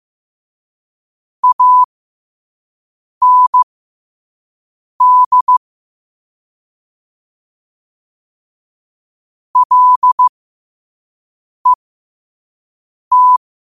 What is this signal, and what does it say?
Morse code, heard as a keyed tone, about 15 wpm, spaced as for 5 wpm, 1000 Hz. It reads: AND LET
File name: qotdfiles / 5wpm